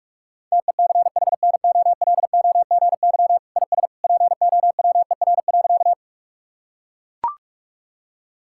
Morse code, heard as a keyed tone, about 42 wpm, 700 Hz. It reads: TECHNOLOGY IS POWER.
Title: Sentences from Top 1000 Words - Review Part 5 - 45wpm